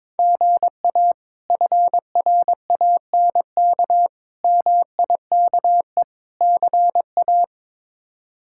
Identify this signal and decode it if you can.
Morse code, heard as a keyed tone, about 22 wpm, 700 Hz. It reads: GA FRANK MIKE CA
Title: K1USN SST QSO Practice - 22wpm